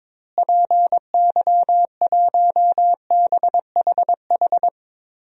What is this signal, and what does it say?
Morse code, heard as a keyed tone, about 22 wpm, 700 Hz. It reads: PY1BHH
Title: DXCC-100 Call Signs - 6 Characters - Repeated 2x - 22wpm